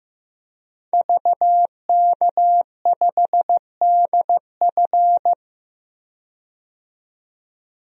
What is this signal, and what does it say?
Morse code, heard as a keyed tone, about 15 wpm, 700 Hz. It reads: VK5DF
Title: DXCC-100 Call Signs - 5 Characters - 15wpm